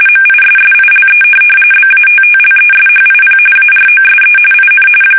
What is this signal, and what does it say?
RTTY, about 45.45 baud. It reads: MISSILE BATTLESHIP AROSE IN
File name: fsk850 / USS-Massachusetts